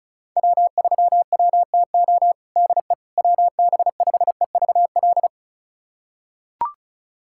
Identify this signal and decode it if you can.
Morse code, heard as a keyed tone, about 36 wpm, 690 Hz. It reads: W3WTO DE WB5EVL
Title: USA Callsign DE Callsign 35wpm